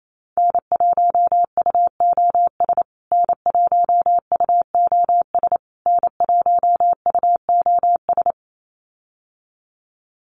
Morse code, heard as a keyed tone, about 28 wpm, 700 Hz. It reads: N1UOS N1UOS N1UOS
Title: USA Call Signs Repeated 3x 28wpm